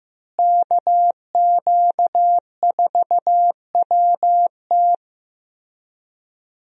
Morse code, heard as a keyed tone, about 15 wpm, 700 Hz. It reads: KQ4WT
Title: USA Call Signs - 5 Characters - Repeated 3x - 15wpm